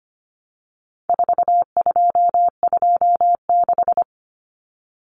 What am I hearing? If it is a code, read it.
Morse code, heard as a keyed tone, about 25 wpm, 700 Hz. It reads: 4226
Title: Common Cut Numbers - 4-Digit 25wpm